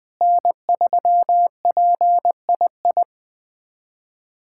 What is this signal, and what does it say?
Morse code, heard as a keyed tone, about 20 wpm, 700 Hz. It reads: N3PII
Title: USA Callsign DE Callsign - 5 Characters - 20wpm